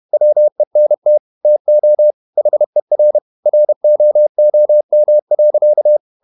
Morse code, heard as a keyed tone, about 31 wpm, 600 Hz. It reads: WENT TO HER ROOM.
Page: Never 31wpm | CW med Gnister